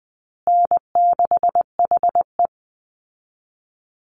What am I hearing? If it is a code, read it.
Morse code, heard as a keyed tone, about 20 wpm, 700 Hz. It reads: N6HE
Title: Easy Call Signs Repeated 2x - 20wpm